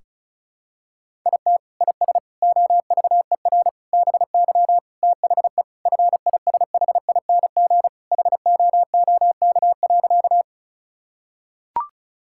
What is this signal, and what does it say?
Morse code, heard as a keyed tone, about 34 wpm, 700 Hz. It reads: IT IS OVER BY THE FISHING HOOK.
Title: Sentences from Top 1600 Words 35wpm